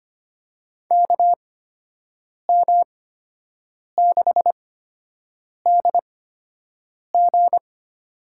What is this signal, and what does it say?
Morse code, heard as a keyed tone, about 25 wpm, spaced as for 7 wpm, 700 Hz. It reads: KM6DG
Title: USA Call Signs - 5 Characters - 8x Character Spacing - 25wpm